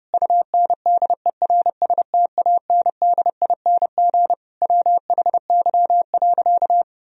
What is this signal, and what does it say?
Morse code, heard as a keyed tone, about 32 wpm, 700 Hz. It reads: UNDERSTANDING WHY.
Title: Sentences from Top 1000 Words - Review Part 3 - 30wpm